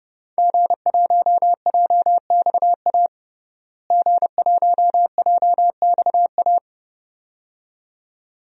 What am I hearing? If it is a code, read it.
Morse code, heard as a keyed tone, about 30 wpm, 700 Hz. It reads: G1JXA G1JXA
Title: DXCC-100 Call Signs - 5 Characters - Repeated 2x - 30wpm